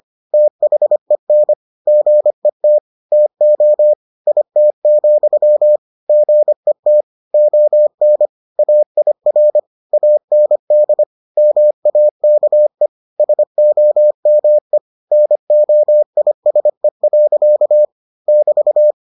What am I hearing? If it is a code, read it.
Morse code, heard as a keyed tone, about 25 wpm, 600 Hz. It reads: THEN GET TO IT, GET ON AIR AND MAKE SOME NOISE. <BT>